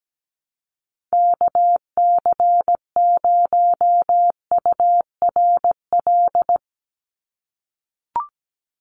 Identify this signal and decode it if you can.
Morse code, heard as a keyed tone, about 17 wpm, 700 Hz. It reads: KC0URL
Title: Call Signs - 17wpm